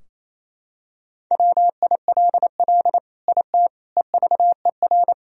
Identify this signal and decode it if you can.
Morse code, heard as a keyed tone, about 28 wpm, 700 Hz. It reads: WILL IT EVER
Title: Sentences from Top 1000 Words - Review Part 2 - 28wpm